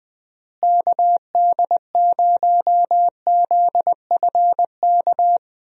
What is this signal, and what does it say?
Morse code, heard as a keyed tone, about 20 wpm, 700 Hz. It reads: KD0ZFK
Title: USA Call Signs - 6 Characters - Repeated 3x - 20wpm